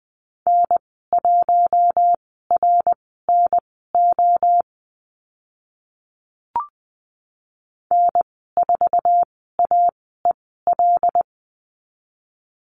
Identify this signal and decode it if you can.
Morse code, heard as a keyed tone, about 20 wpm, spaced as for 14 wpm, 700 Hz. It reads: N1RNO N4AEL
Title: 076 Lesson 13g - Letter L - Call Signs 20wpm